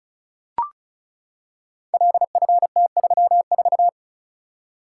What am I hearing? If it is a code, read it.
Morse code, heard as a keyed tone, about 36 wpm, 700 Hz. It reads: LFT34
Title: Code Groups - 5 Letter:Numbers - 35wpm